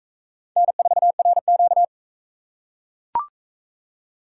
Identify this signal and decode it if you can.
Morse code, heard as a keyed tone, about 41 wpm, 700 Hz. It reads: N4RQ